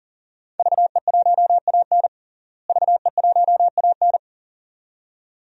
Morse code, heard as a keyed tone, about 40 wpm, 700 Hz. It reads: VE1AN VE1AN
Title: DXCC-100 Call Signs - 5 Characters - Repeated 2x - 40wpm